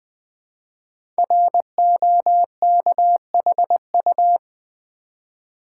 Morse code, heard as a keyed tone, about 20 wpm, 700 Hz. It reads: ROKHU
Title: Code Groups - 5 Letters - 20wpm